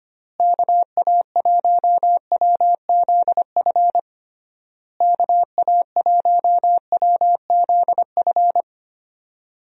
Morse code, heard as a keyed tone, about 25 wpm, 700 Hz. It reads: KA1WZF KA1WZF
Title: USA Call Signs - 6 Characters - Repeated 3x - 25wpm